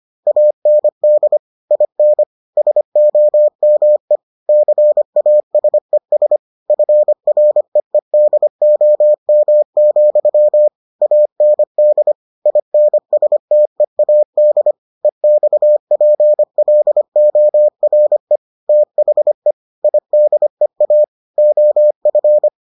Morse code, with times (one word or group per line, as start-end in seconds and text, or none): 0.3-1.4 AND
1.7-2.2 IN
2.6-4.2 SOME
4.5-6.4 CASES
6.7-10.7 FREEDOM,
11.0-12.1 AND
12.5-14.7 INSTEAD
15.0-18.4 EXPLORE
18.7-19.5 THE
19.8-21.0 IDEA
21.4-22.5 OF